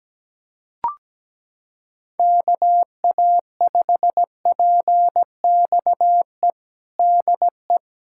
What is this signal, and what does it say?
Morse code, heard as a keyed tone, about 17 wpm, 700 Hz. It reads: KA5PXE DE
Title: USA Callsign DE Callsign 17wpm